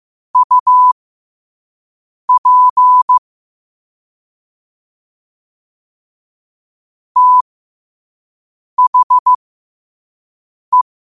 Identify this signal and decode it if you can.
Morse code, heard as a keyed tone, about 15 wpm, spaced as for 5 wpm, 1000 Hz. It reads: UP THE
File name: qotdfiles / 5wpm